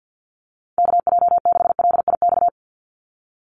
Morse code, heard as a keyed tone, about 48 wpm, 700 Hz. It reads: KJ6LIX
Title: USA Call Signs 50wpm